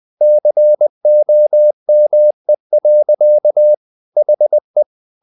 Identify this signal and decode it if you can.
Morse code, heard as a keyed tone, about 20 wpm, 600 Hz. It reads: COME. HE
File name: the old man and the sea part 2 20wpm